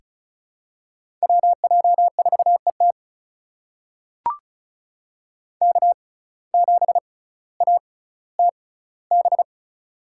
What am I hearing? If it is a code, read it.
Morse code, heard as a keyed tone, about 36 wpm, spaced as for 18 wpm, 700 Hz. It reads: WJ4ET K7ATB
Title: USA Call Signs - 5 Characters - 6x Character Spacing - 35wpm